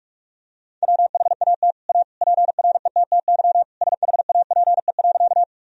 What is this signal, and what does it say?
Morse code, heard as a keyed tone, about 43 wpm, 700 Hz. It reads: WHAT A PRETTY SHAPE.
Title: Sentences from Top 1000 Words - Review Part 3 - 45wpm